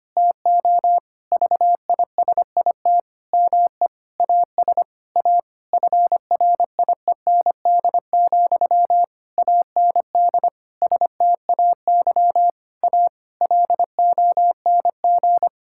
Morse code, heard as a keyed tone, about 25 wpm, 700 Hz. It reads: TO VISIT ME AS A FRIEND, AND STAY A LONG
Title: Sentences from Top 1000 Words - Review Part 6 - 25wpm